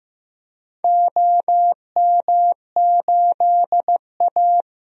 Morse code, heard as a keyed tone, about 15 wpm, 700 Hz. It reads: OM8A
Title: DXCC-100 Call Signs - 4 Characters - Repeated 2x - 15wpm